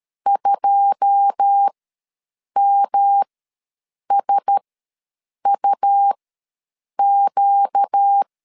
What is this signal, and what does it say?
Morse code, heard as a keyed tone, about 13 wpm, spaced as for 7 wpm, 800 Hz. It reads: 2MSUQ
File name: LEZIONI / 9 286 ws 12wpm 3351